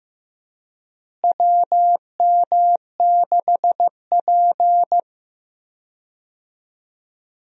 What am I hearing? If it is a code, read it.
Morse code, heard as a keyed tone, about 15 wpm, 700 Hz. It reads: WM6P
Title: USA Call Signs - 4 Characters - Repeated 3x - 15wpm